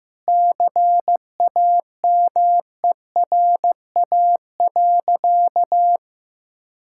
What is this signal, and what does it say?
Morse code, heard as a keyed tone, about 15 wpm, 700 Hz. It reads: CAMERA.
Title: Sentences from Top 1000 Words - Review Part 5 - 15wpm